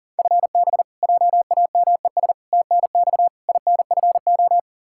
Morse code, heard as a keyed tone, about 40 wpm, 700 Hz. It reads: FB JAMES TNX INFO
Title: Practice QSO - Exchange 3 - More Info - 40wpm